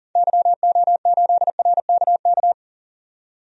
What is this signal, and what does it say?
Morse code, heard as a keyed tone, about 40 wpm, 700 Hz. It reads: YO8RKK
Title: DXCC-100 Call Signs - 6 Characters - 40wpm